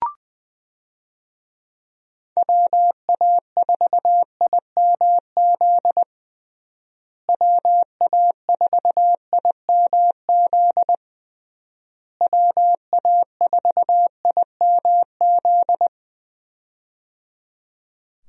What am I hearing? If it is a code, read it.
Morse code, heard as a keyed tone, about 20 wpm, 700 Hz. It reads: WA4IMZ WA4IMZ WA4IMZ
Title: USA Call Signs - 6 Characters - Repeated 3x - 20wpm